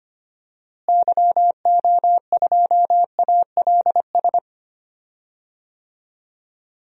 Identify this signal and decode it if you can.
Morse code, heard as a keyed tone, about 25 wpm, 700 Hz. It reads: YO2ALS
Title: DXCC-100 Call Signs - 6 Characters - Repeated 3x - 25wpm